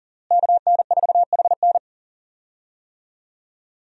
Morse code, heard as a keyed tone, about 40 wpm, 700 Hz. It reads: KN4HN
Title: USA Call Signs - 5 Characters - Repeated 3x - 40wpm